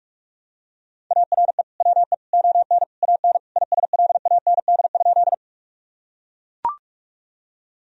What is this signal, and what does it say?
Morse code, heard as a keyed tone, about 42 wpm, 700 Hz. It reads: ARE WE ON AN ISLAND?
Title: Sentences from Top 1000 Words - Review Part 3 - 45wpm